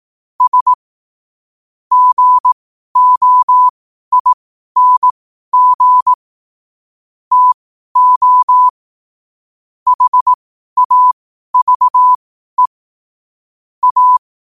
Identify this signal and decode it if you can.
Morse code, heard as a keyed tone, about 18 wpm, spaced as for 12 wpm, 1000 Hz. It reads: S GOING TO HAVE A
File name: qotdfiles / 12wpm